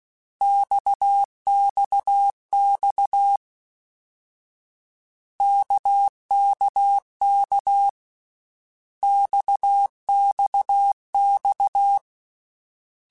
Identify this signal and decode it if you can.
Morse code, heard as a keyed tone, about 16 wpm, 800 Hz. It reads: XXX KKK XXX
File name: Basic Letter Groups By7Groups / RKPX / RKPX@13